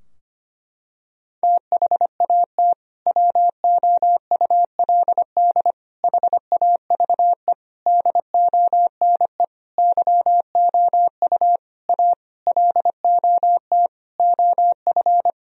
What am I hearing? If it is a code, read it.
Morse code, heard as a keyed tone, about 25 wpm, 700 Hz. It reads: THAT WOULD HAVE DONE YOU A LOT OF
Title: Sentences from Top 1000 Words - Review Part 3 - 25wpm